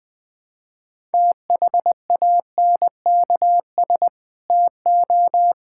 Morse code, heard as a keyed tone, about 20 wpm, 700 Hz. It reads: THANKS TO
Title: Sentences from Top 1000 Words - Review Part 5 - 20wpm